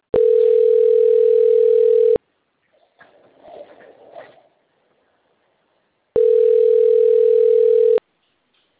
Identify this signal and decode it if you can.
Ringback tone (North America), 440 + 480 Hz.